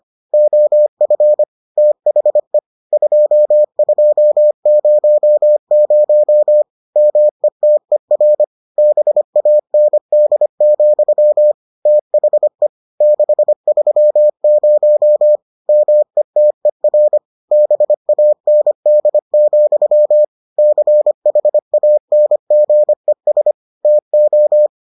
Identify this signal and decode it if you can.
Morse code, heard as a keyed tone, about 25 wpm, 600 Hz. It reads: OF THE 2200 METER BAND, THE 630 METER BAND, CHANGES TO